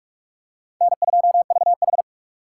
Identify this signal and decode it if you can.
Morse code, heard as a keyed tone, about 45 wpm, 700 Hz. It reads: N2VH
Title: USA Call Signs - 4 Characters - 3x Character Spacing - 45wpm